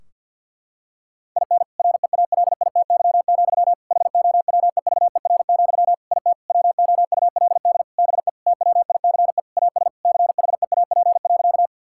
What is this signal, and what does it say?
Morse code, heard as a keyed tone, about 50 wpm, 700 Hz. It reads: IN REALITY, HOWEVER, IT WOULD BE TWICE AS CHEAP.
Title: Sentences from Top 1600 Words 50wpm